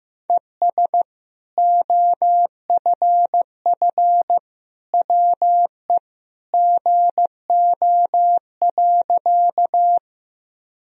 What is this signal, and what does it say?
Morse code, heard as a keyed tone, about 15 wpm, 700 Hz. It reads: ES OFF WE GO.
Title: Sentences from Top 200 Words and Abbreviations 15wpm